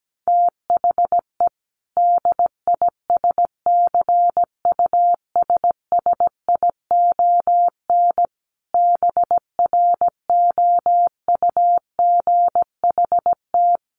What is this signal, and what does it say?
Morse code, heard as a keyed tone, about 17 wpm, 700 Hz. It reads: THE DISCUSSION BROUGHT
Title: Sentences from Top 1100 Words 17wpm